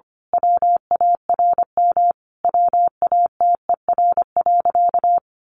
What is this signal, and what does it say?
Morse code, heard as a keyed tone, about 25 wpm, 700 Hz. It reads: WARM WATER.
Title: A Sound Of Thunder 25wpm